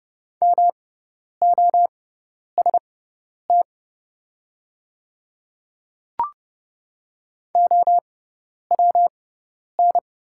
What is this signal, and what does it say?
Morse code, heard as a keyed tone, about 30 wpm, spaced as for 10 wpm, 700 Hz. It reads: MOST OWN
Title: 6x Character Spacing - Top 300 Words - 30wpm